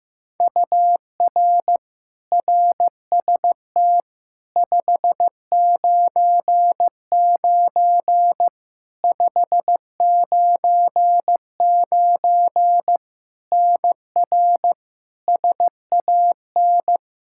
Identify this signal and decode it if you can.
Morse code, heard as a keyed tone, about 15 wpm, 700 Hz. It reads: UR RST 599 599 NR SAN
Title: Practice QSO - Exchange 1 - Beginning - 15wpm